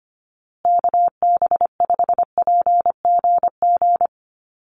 Morse code, heard as a keyed tone, about 25 wpm, 700 Hz. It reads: KB5PGG
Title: USA Call Signs Repeated 2x 25wpm